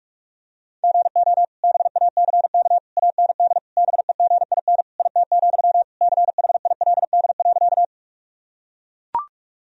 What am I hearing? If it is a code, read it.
Morse code, heard as a keyed tone, about 41 wpm, 700 Hz. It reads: GO BACK AND BEGIN IT, CHILD.
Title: Sentences from Top 1000 Words - Review Part 1 - 45wpm